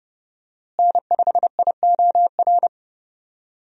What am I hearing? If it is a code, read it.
Morse code, heard as a keyed tone, about 30 wpm, 700 Hz. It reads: N5IOR
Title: USA Call Signs - 5 Characters - Repeated 3x - 30wpm